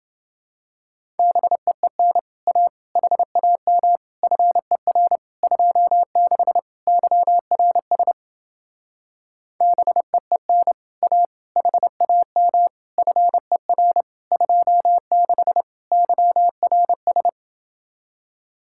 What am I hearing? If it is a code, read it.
Morse code, heard as a keyed tone, about 29 wpm, 700 Hz. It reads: BEEN A HAM FER 26 YRS BEEN A HAM FER 26 YRS